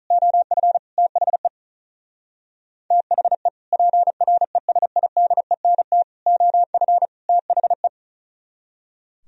Morse code, heard as a keyed tone, about 45 wpm, 700 Hz. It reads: OF THE THE PRESIDENT OF THE